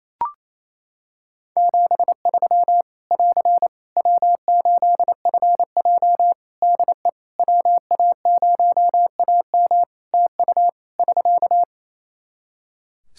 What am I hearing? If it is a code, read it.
Morse code, heard as a keyed tone, about 27 wpm, 700 Hz. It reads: 73 <AR> W8FJ DE WA0AM TU <SK>
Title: Practice QSO - Exchange 4 - Ending - 28wpm